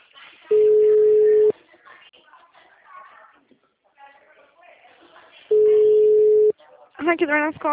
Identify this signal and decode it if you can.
Ringback tone (Europe, 425 Hz), with speech.